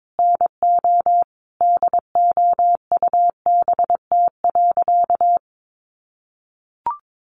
Morse code, heard as a keyed tone, about 22 wpm, 700 Hz. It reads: NO DOUBT.
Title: Sentences from Top 1500 Words 22wpm